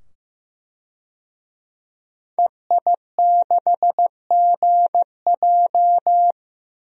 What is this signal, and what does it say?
Morse code, heard as a keyed tone, about 15 wpm, 700 Hz. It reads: EI6GJ
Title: DXCC-100 Call Signs - 5 Characters - 15wpm